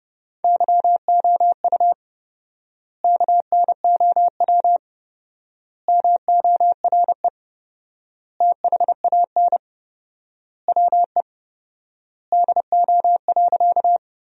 Morse code, heard as a keyed tone, about 30 wpm, 700 Hz. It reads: YOU KNOW MORE THAN WE DO.
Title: 4x Word-Spacing - Sentences from Top 100 Words - 30wpm